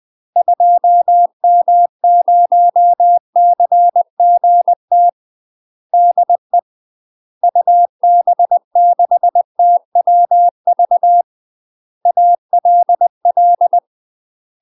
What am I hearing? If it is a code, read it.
Morse code, heard as a keyed tone, about 20 wpm, 700 Hz. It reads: 2M0CGT DE UB6TWV ALL
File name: UK MASTER QSOs TRAINING / 20WPM / UK MASTER QSOs 0068